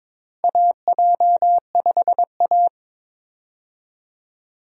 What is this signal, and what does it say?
Morse code, heard as a keyed tone, about 22 wpm, 700 Hz. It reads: AJ5A
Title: USA Call Signs - 4 Characters - Repeated 2x - 22wpm